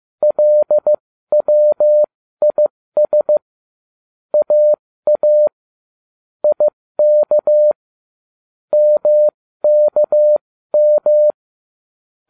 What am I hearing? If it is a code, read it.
Morse code, heard as a keyed tone, about 15 wpm, spaced as for 13 wpm, 600 Hz. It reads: LWIS AA IK MKM